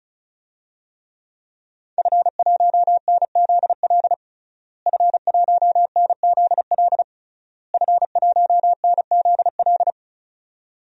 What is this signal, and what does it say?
Morse code, heard as a keyed tone, about 36 wpm, 700 Hz. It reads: F1NZL F1NZL F1NZL
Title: DXCC-100 Call Signs - 5 Characters - Repeated 3x - 35wpm